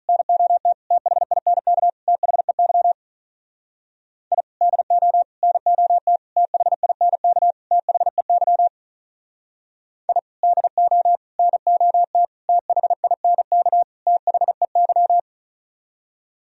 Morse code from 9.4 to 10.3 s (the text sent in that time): I